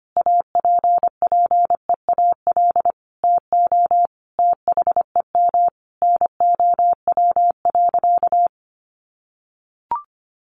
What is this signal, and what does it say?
Morse code, heard as a keyed tone, about 25 wpm, 700 Hz. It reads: APPEAL TO THEM NOW.
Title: Sentences from Top 1300 Words 25wpm